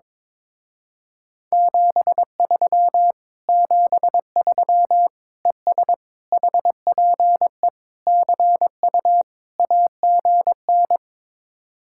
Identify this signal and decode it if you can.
Morse code, heard as a keyed tone, about 22 wpm, 700 Hz. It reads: 73 73 ES HPE CU AGN
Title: Speed-Racing Common QSO Phrases - 22wpm